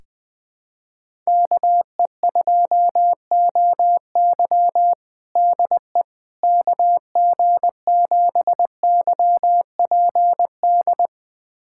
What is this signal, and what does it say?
Morse code, heard as a keyed tone, about 20 wpm, 700 Hz. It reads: KE2OY DE KG7YPD
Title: USA Callsign DE Callsign 20wpm